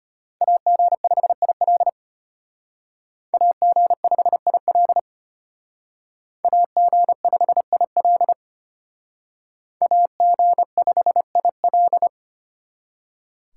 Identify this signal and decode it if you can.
Morse code, heard as a keyed tone, about 32 wpm, 700 Hz. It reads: AG5IL AG5IL AG5IL AG5IL